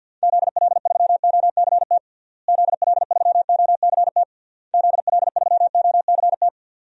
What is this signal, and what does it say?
Morse code, heard as a keyed tone, about 52 wpm, 700 Hz. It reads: ZL3OCT ZL3OCT ZL3OCT
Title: DXCC-100 Call Signs - 6 Characters - Repeated 3x - 50wpm